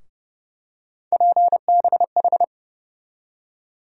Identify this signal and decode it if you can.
Morse code, heard as a keyed tone, about 30 wpm, 700 Hz. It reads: PBH
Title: USA Call Sign Postfixes - 3 Letters 30wpm